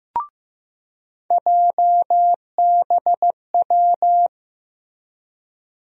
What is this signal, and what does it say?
Morse code, heard as a keyed tone, about 15 wpm, 700 Hz. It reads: JBW
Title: USA Call Sign Postfixes - 3 Letters 15wpm